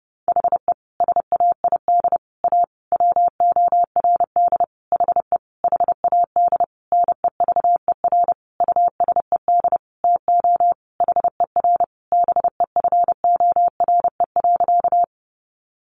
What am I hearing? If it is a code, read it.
Morse code, heard as a keyed tone, about 30 wpm, 700 Hz. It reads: HE SAID A WORD HE HAD NEVER USED TO HER BEFORE.
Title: Sentences from Top 600 Words 30wpm